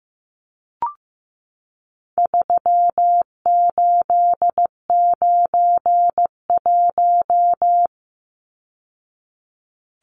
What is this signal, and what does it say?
Morse code, heard as a keyed tone, about 15 wpm, 700 Hz. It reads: 3891
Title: Numbers - 4-Digit 15wpm